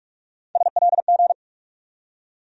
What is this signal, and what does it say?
Morse code, heard as a keyed tone, about 45 wpm, 700 Hz. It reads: SLG
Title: USA Call Sign Postfixes - 3 Letters 45wpm